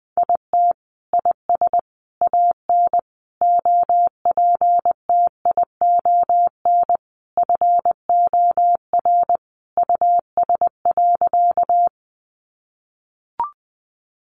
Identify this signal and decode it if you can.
Morse code, heard as a keyed tone, about 20 wpm, 700 Hz. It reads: IT IS AN OPTION FOR US.
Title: Sentences from Top 1300 Words 20wpm